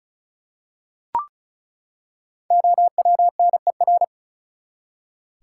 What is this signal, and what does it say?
Morse code, heard as a keyed tone, about 36 wpm, 700 Hz. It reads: OWNER
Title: Five Letter Words - 35wpm